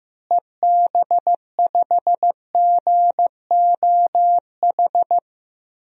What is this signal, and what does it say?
Morse code, heard as a keyed tone, about 15 wpm, 700 Hz. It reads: EB5GOH
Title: DXCC-100 Call Signs - 6 Characters - Repeated 2x - 15wpm